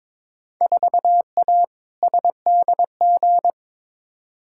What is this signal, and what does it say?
Morse code, heard as a keyed tone, about 22 wpm, 700 Hz. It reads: 4A SDG
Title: Field Day Exchanges - 22wpm